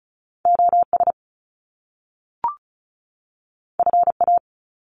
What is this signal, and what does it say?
Morse code, heard as a keyed tone, about 36 wpm, 700 Hz. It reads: OS FA
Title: Top Two Letter Combinations in Words 35wpm